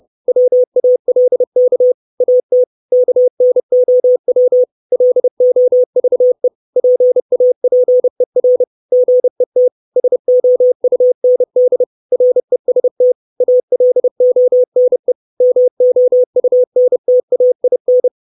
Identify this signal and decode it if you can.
Morse code, heard as a keyed tone, about 30 wpm, 500 Hz. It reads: WALK AT KNOW LOVE PAPER GET SOUND REST ALONE MOUNTAIN
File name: 632 Words 30WPM